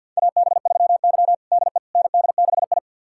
Morse code, heard as a keyed tone, about 51 wpm, 700 Hz. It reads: AB3Y DE ND6I
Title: USA Callsign DE Callsign - 4 Characters - 50wpm